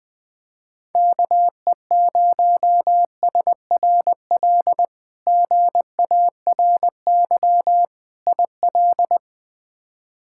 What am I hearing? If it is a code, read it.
Morse code, heard as a keyed tone, about 20 wpm, 700 Hz. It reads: KE0SRL GARY IL
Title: K1USN SST QSO Practice - 20wpm